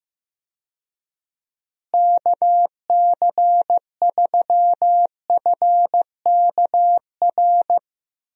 Morse code, heard as a keyed tone, about 15 wpm, 700 Hz. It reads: KC3FKR
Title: USA Call Signs - 6 Characters - Repeated 2x - 15wpm